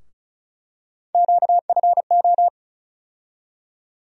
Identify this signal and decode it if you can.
Morse code, heard as a keyed tone, about 34 wpm, 700 Hz. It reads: QFO
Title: USA Call Sign Postfixes - 3 Letters 35wpm